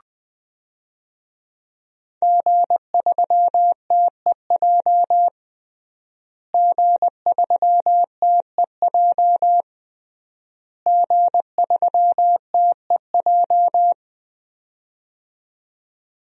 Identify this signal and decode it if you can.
Morse code, heard as a keyed tone, about 20 wpm, 700 Hz. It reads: G3TEJ G3TEJ G3TEJ
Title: DXCC-100 Call Signs - 5 Characters - Repeated 3x - 20wpm